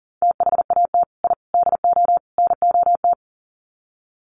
Morse code, heard as a keyed tone, about 40 wpm, 700 Hz. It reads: THAT I DO NOT
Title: Sets of 4 Words 40wpm